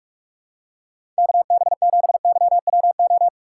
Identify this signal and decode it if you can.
Morse code, heard as a keyed tone, about 43 wpm, 700 Hz. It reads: KB7YWO
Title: USA Call Signs - 2x Character Spacing - 45wpm